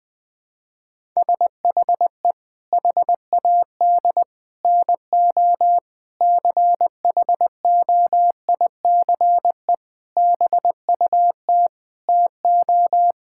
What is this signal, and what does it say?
Morse code, heard as a keyed tone, about 21 wpm, 700 Hz. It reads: SHE HAD NO CHOICE BUT TO
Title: Sentences with 5-Word Phrases - 20wpm